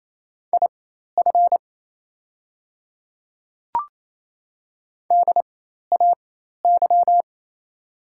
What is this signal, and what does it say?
Morse code, heard as a keyed tone, about 28 wpm, spaced as for 13 wpm, 700 Hz. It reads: IF DAY
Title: 4x Character Spacing - Top 100 Words - 28wpm